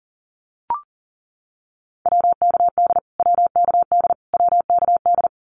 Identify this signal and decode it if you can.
Morse code, heard as a keyed tone, about 40 wpm, 700 Hz. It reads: WKD WKD WKD
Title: Abbreviations Repeated 3x - 40wpm